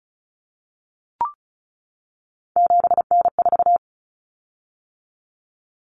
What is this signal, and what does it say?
Morse code, heard as a keyed tone, about 34 wpm, 700 Hz. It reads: 7N4
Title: Common Cut Numbers - 3-Digit 35wpm